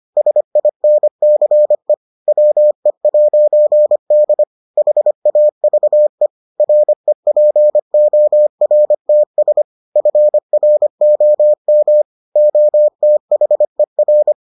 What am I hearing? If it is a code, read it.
Morse code, heard as a keyed tone, about 25 wpm, 600 Hz. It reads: SINCE WE'D HAVE REPORTS FROM OTHER